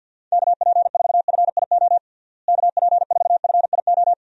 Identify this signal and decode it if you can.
Morse code, heard as a keyed tone, about 50 wpm, 700 Hz. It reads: KP4FIO KP4FIO
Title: USA Call Signs - 6 Characters - Repeated 3x - 50wpm